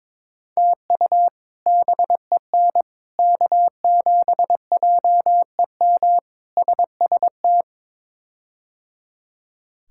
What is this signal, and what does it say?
Morse code, heard as a keyed tone, about 22 wpm, 700 Hz. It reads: TU BEN K7JEM SST
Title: K1USN SST QSO Practice - 22wpm